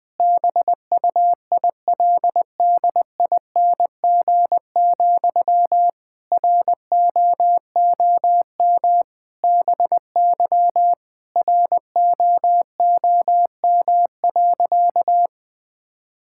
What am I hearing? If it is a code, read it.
Morse code, heard as a keyed tone, about 20 wpm, 700 Hz. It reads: BUILDING, ROOM BY ROOM.
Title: Sentences from Top 1000 Words - Review Part 5 - 20wpm